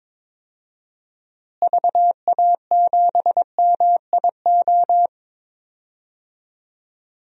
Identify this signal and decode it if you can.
Morse code, heard as a keyed tone, about 22 wpm, 700 Hz. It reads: VA7MIO
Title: DXCC-100 Call Signs - 6 Characters - Repeated 3x - 22wpm